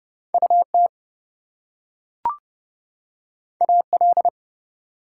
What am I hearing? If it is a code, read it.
Morse code, heard as a keyed tone, about 32 wpm, 700 Hz. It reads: UT AL
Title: US States and Canadian Provinces 30wpm